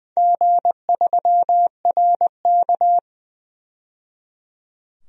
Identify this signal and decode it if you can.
Morse code, heard as a keyed tone, about 20 wpm, 700 Hz. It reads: G3RK
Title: DXCC-100 Call Signs - 4 Characters - 20wpm